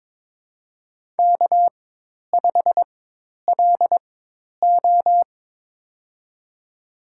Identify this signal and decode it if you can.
Morse code, heard as a keyed tone, about 22 wpm, spaced as for 10 wpm, 700 Hz. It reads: K5LO
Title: USA Call Signs - 4 Characters - 4x Character Spacing - 22wpm